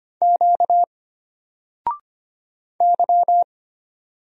Morse code, heard as a keyed tone, about 25 wpm, 700 Hz. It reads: Q Y
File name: Single-Letter-Number-25wpm